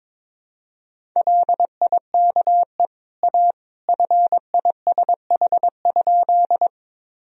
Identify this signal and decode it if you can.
Morse code, heard as a keyed tone, about 22 wpm, 700 Hz. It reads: LIKE A FISH?
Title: Sentences from Top 1000 Words - Review Part 1 - 22wpm